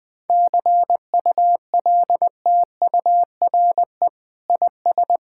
Morse code, heard as a keyed tone, about 21 wpm, 700 Hz. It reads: CULTURE IS
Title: Sentences from Top 1000 Words - Review Part 6 - 20wpm